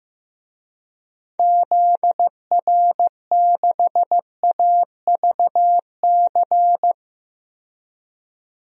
Morse code, heard as a keyed tone, about 15 wpm, 700 Hz. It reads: ZR6AVC
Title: DXCC-100 Call Signs - 6 Characters - 15wpm